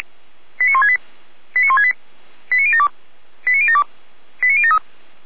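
Five-tone selective call (ZVEI), heard as quick runs of five tones, tones of about 70 ms. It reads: 89168 89168 89072 89072 89073
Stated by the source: received on an SDR; ZVEI five-tone call 89150, 455 directory.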